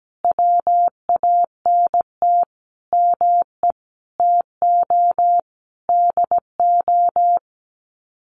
Morse code, heard as a keyed tone, about 17 wpm, 700 Hz. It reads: WANT ME TO DO
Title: Sets of 5 Words Encore2 17wpm